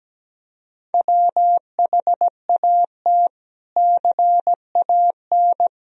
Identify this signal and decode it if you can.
Morse code, heard as a keyed tone, about 17 wpm, 700 Hz. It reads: WHAT CAN
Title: Sentences from Top 1000 Words - Review Part 4 - 17wpm